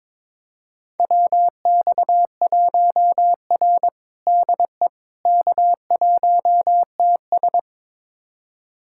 Morse code, heard as a keyed tone, about 22 wpm, 700 Hz. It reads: WX1R DE K1TS
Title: USA Callsign DE Callsign - 4 Characters - 22wpm